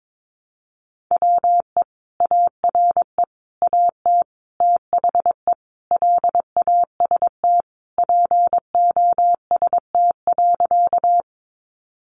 Morse code, heard as a keyed tone, about 22 wpm, 700 Hz. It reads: WE ARE AT THE LAST POST.
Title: Sentences from Top 1000 Words 22wpm